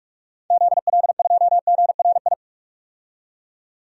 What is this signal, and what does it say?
Morse code, heard as a keyed tone, about 45 wpm, 700 Hz. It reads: ZL2GRI
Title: DXCC-100 Call Signs - 6 Characters - 45wpm